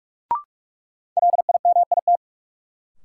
Morse code, heard as a keyed tone, about 42 wpm, 700 Hz. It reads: LIMIT
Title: Sentences from Top 1500 Words - New Word Drill - 45wpm